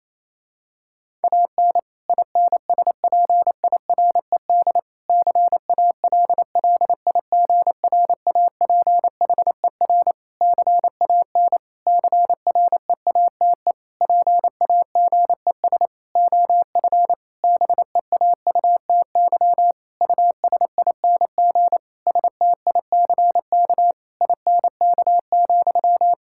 Morse code, heard as a keyed tone, about 28 wpm, 700 Hz. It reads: AN INSPIRED CALLIGRAPHER CAN CREATE PAGES OF BEAUTY USING STICK INK,
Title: Pangram Challenge v2 - 28wpm